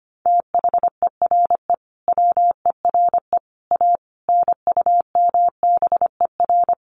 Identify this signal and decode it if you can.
Morse code, heard as a keyed tone, about 25 wpm, 700 Hz. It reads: THERE WERE A NUMBER
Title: Sets of 5 Words Encore2 25wpm